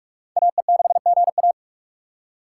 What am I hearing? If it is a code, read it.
Morse code, heard as a keyed tone, about 40 wpm, 700 Hz. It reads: AE6GA